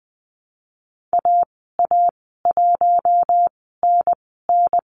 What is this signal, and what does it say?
Morse code, heard as a keyed tone, about 20 wpm, spaced as for 14 wpm, 700 Hz. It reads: AA1NN
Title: 042 Lesson 8g - Number 1 - Call Signs 20wpm